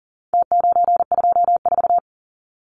Morse code, heard as a keyed tone, about 40 wpm, 700 Hz. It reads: T924
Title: Numbers - Contest Style - 4-Digit 40wpm